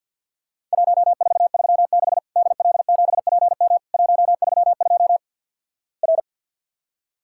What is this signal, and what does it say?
Morse code, heard as a keyed tone, about 50 wpm, 700 Hz. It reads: 1436 DL7PM 132 R
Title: Worked All Europe DX Contest - QTC Exchange - 50wpm